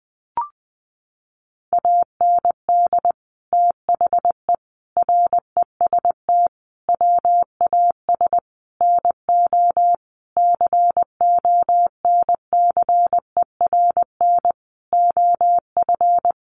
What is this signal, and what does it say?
Morse code, heard as a keyed tone, about 20 wpm, 700 Hz. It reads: AND THE REST WAS NO CONCERN OF
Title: Sentences from Top 1000 Words 20wpm